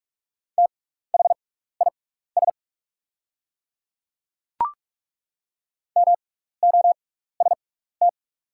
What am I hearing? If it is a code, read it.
Morse code, heard as a keyed tone, about 43 wpm, spaced as for 15 wpm, 690 Hz. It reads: THIS MOST
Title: 6x Character Spacing - Top 100 Words - 45wpm